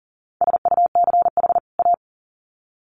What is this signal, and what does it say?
Morse code, heard as a keyed tone, about 40 wpm, 700 Hz. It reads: SUCH A
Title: Sets of 2 Words 40wpm